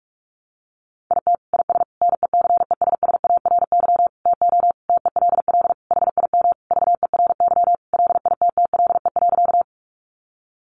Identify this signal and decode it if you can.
Morse code, heard as a keyed tone, about 44 wpm, 700 Hz. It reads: IT IS NECESSARY TO TELL HIM VERY LITTLE.
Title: Sentences from Top 800 Words 45wpm